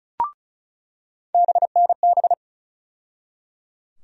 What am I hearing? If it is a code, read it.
Morse code, heard as a keyed tone, about 35 wpm, 700 Hz. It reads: BNB
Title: USA Call Sign Postfixes - 3 Letters 35wpm